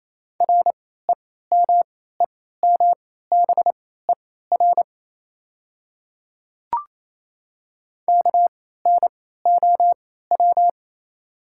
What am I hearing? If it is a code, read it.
Morse code, heard as a keyed tone, about 28 wpm, spaced as for 16 wpm, 700 Hz. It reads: REMEMBER KNOW
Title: 3x Character Spacing - Top 500 Words - 28wpm